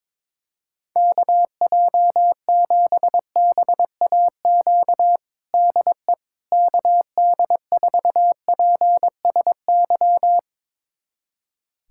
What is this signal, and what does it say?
Morse code, heard as a keyed tone, about 22 wpm, 700 Hz. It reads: KJ7BAQ DE KD4PSY
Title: USA Callsign DE Callsign 22wpm